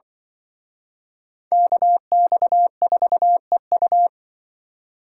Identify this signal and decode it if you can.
Morse code, heard as a keyed tone, about 24 wpm, 700 Hz. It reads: KX4EU
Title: Speed Racing - USA 2x2 Call Signs 20wpm